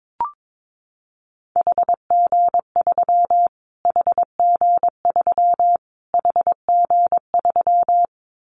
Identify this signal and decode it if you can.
Morse code, heard as a keyed tone, about 22 wpm, 700 Hz. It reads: HG3 HG3 HG3
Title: Callsign Prefixes for Top 25 DXCC - Repeated 3x - 22wpm